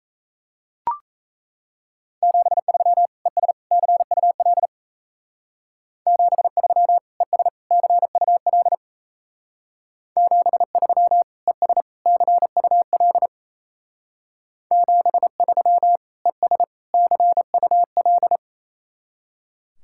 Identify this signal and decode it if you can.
Morse code, heard as a keyed tone, about 36 wpm, 700 Hz. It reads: 73 ES CUL 73 ES CUL 73 ES CUL 73 ES CUL